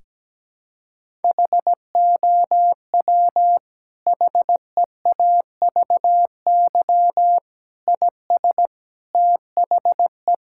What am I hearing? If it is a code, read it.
Morse code, heard as a keyed tone, about 17 wpm, 700 Hz. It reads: HOW HEAVY IS THE
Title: Sentences from Top 1000 Words - Review Part 6 - 17wpm